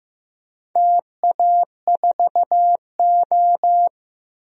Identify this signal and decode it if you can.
Morse code, heard as a keyed tone, about 16 wpm, 700 Hz. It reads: TA4O
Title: DXCC-100 Call Signs - 4 Characters - Repeated 3x - 15wpm